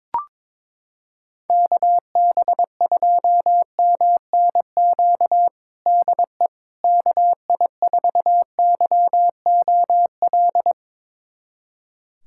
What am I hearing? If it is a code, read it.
Morse code, heard as a keyed tone, about 21 wpm, 700 Hz. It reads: KB2MNQ DE KI4YOL
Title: USA Callsign DE Callsign - 6 Characters - 22wpm